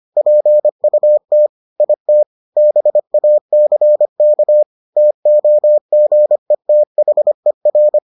Morse code, heard as a keyed tone, about 25 wpm, 600 Hz. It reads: PUT IT BACK TOGETHER